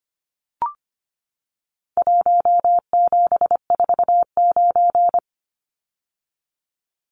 Morse code, heard as a keyed tone, about 25 wpm, 700 Hz. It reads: 1749
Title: Numbers - Contest Style - 4-Digit 25wpm